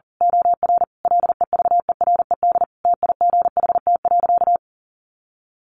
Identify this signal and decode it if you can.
Morse code, heard as a keyed tone, about 40 wpm, 700 Hz. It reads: OR LEVERED TIGHT.
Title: A Sound Of Thunder 40wpm